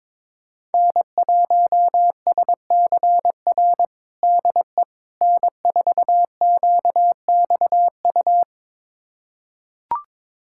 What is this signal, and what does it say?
Morse code, heard as a keyed tone, about 22 wpm, 700 Hz. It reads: N1SCR DE N4QXU
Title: USA Callsign DE Callsign - 5 Characters - 22wpm